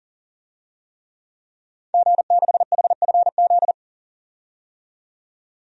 Morse code, heard as a keyed tone, about 40 wpm, 700 Hz. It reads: G6HFZ
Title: DXCC-100 Call Signs - 5 Characters - 40wpm